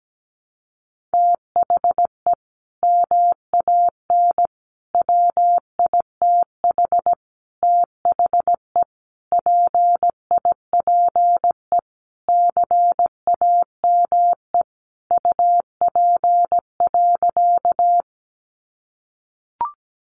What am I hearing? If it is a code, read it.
Morse code, heard as a keyed tone, about 17 wpm, 700 Hz. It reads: THE MAN WITH THE PIPE CAME UP.
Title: Sentences from Top 1500 Words 17wpm